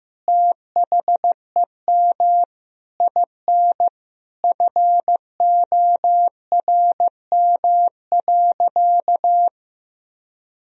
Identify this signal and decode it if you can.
Morse code, heard as a keyed tone, about 15 wpm, 700 Hz. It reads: THEM IN FORM.
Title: Sentences from Top 1000 Words - Review Part 1 - 15wpm